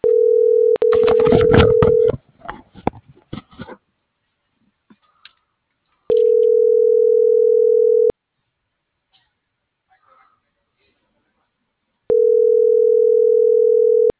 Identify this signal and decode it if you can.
Ringback tone (North America), 440 + 480 Hz.
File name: CENTRAL BLN / out ximena.diaz